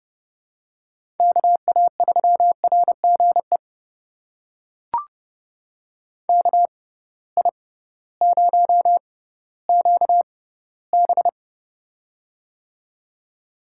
Morse code, heard as a keyed tone, about 30 wpm, spaced as for 16 wpm, 700 Hz. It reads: KA3RGE KI0QB
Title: USA Call Signs - 6x Character Spacing - 30wpm